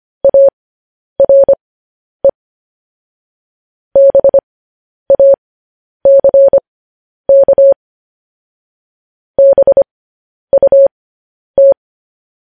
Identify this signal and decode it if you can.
Morse code, heard as a keyed tone, about 25 wpm, spaced as for 10 wpm, 550 Hz. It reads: ARE BACK BUT
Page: news headlines in morse code at 10 wpm